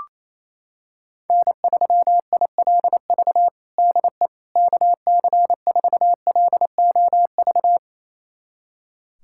Morse code, heard as a keyed tone, about 28 wpm, 700 Hz. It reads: N3ILV DE KC4LOV
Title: USA Callsign DE Callsign 28wpm